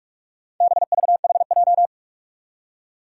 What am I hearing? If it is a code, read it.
Morse code, heard as a keyed tone, about 44 wpm, 700 Hz. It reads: BVHJ
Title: Four Challenging Letter-Number Combinations 45wpm